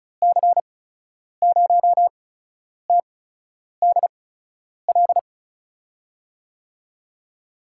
Morse code, heard as a keyed tone, about 34 wpm, spaced as for 10 wpm, 700 Hz. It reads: C0TDL
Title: USA Call Signs - 6 Characters - 8x Character Spacing - 35wpm